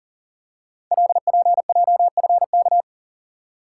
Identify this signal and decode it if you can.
Morse code, heard as a keyed tone, about 42 wpm, 690 Hz. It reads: LPJFK
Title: Code Groups - 5 Letters - 40wpm